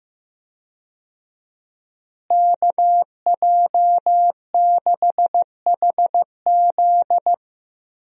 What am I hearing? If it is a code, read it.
Morse code, heard as a keyed tone, about 15 wpm, 700 Hz. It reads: KJ6HZ
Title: USA Call Signs - 5 Characters - Repeated 2x - 15wpm